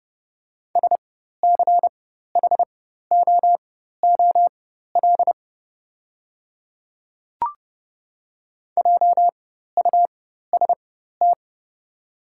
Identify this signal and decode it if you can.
Morse code, heard as a keyed tone, about 31 wpm, spaced as for 14 wpm, 700 Hz. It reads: SCHOOL JUST
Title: 4x Character Spacing - Top 300 Words - 30wpm